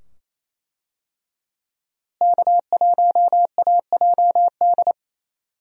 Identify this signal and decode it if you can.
Morse code, heard as a keyed tone, about 27 wpm, 700 Hz. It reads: K1AJD
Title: USA Call Signs - 5 Characters - Repeated 2x - 28wpm